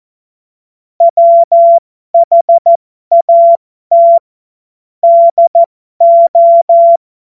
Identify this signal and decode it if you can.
Morse code, heard as a keyed tone, about 14 wpm, spaced as for 12 wpm, 680 Hz. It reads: WHAT DO
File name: Morsekurs / Witze